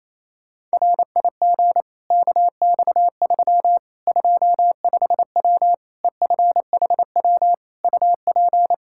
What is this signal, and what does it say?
Morse code, heard as a keyed tone, about 28 wpm, 700 Hz. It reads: RIG KX3 25W EFHW UP
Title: Practice QSO - Exchange 2 - Info - 28wpm